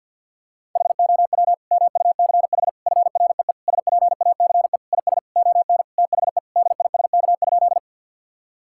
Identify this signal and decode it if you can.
Morse code, heard as a keyed tone, about 50 wpm, 700 Hz. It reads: HOW MUCH FREE SPACE IS ON THE DISK?
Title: Sentences from Top 1000 Words - Review Part 6 - 50wpm